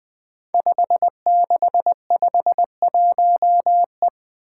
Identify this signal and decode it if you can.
Morse code, heard as a keyed tone, about 20 wpm, 700 Hz. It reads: 5651E
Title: Five Challenging Letter-Number Combinations 20wpm